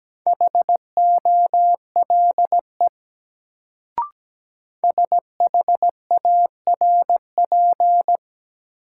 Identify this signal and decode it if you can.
Morse code, heard as a keyed tone, about 17 wpm, 700 Hz. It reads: HOLE SHARP
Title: Sentences from Top 1600 Words - New Word Drill Alt - 17wpm